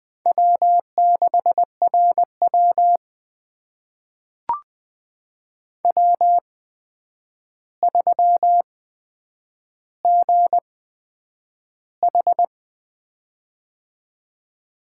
Morse code, heard as a keyed tone, about 21 wpm, spaced as for 9 wpm, 700 Hz. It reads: W6RW W3GH
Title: USA Call Signs - 4 Characters - 8x Character Spacing - 20wpm